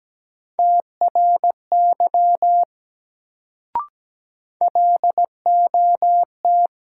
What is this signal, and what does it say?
Morse code, heard as a keyed tone, about 17 wpm, 700 Hz. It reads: TRY LOT
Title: Sets of 2 Words - New Word Drill - 17wpm